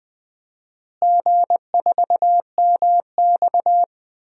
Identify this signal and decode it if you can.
Morse code, heard as a keyed tone, about 20 wpm, 700 Hz. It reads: G4MX
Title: DXCC-100 Call Signs - 4 Characters - Repeated 3x - 20wpm